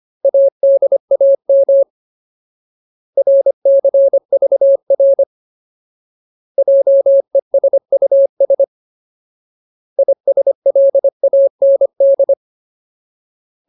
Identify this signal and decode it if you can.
Morse code, heard as a keyed tone, about 25 wpm, 550 Hz. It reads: ADAM RCVR JESUS ISLAND
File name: Ultimate Word List / 25WPM Ultimate